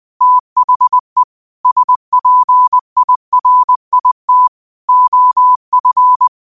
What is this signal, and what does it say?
Morse code, heard as a keyed tone, about 20 wpm, 1000 Hz. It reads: THE SPIRIT OF
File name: qotdfiles / 20wpm